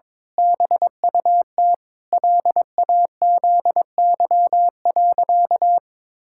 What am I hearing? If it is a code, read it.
Morse code, heard as a keyed tone, about 22 wpm, 700 Hz. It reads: BUT LAZY.
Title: Pangram Challenge v2 - 22wpm